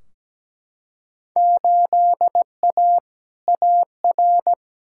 Morse code, heard as a keyed tone, about 17 wpm, 700 Hz. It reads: 8A AR
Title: Field Day Exchanges - 17wpm